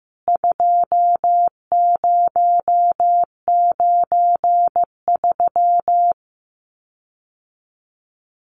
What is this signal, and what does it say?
Morse code, heard as a keyed tone, about 15 wpm, 700 Hz. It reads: 2093
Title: CWT Exchange 15wpm